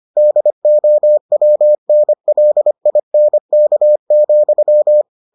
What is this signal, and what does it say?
Morse code, heard as a keyed tone, about 25 wpm, 600 Hz. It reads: DOWNLINK,